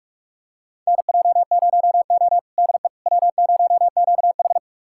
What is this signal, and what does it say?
Morse code, heard as a keyed tone, about 46 wpm, 700 Hz. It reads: NJ0O DE W0QH
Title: USA Callsign DE Callsign - 4 Characters - 45wpm